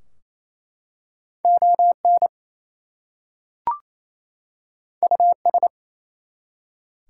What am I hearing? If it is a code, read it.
Morse code, heard as a keyed tone, about 27 wpm, 700 Hz. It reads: ON US
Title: Two Letter Words - 28wpm